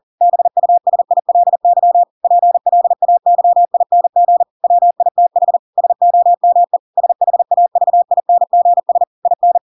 Morse code, heard as a keyed tone, about 40 wpm, 700 Hz. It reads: BUSILY PLAYING WITH SOME SHAVINGS IN
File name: Pinocchio / Pinocchio ch7 40 40 ik1hge